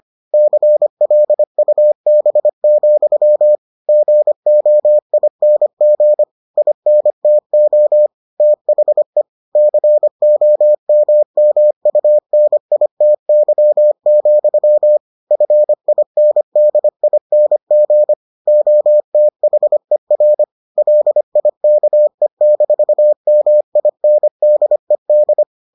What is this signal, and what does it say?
Morse code, heard as a keyed tone, about 25 wpm, 600 Hz. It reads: CLUB, GOING INTO THE COMMUNITY, FINDING OTHER LIKE-MINDED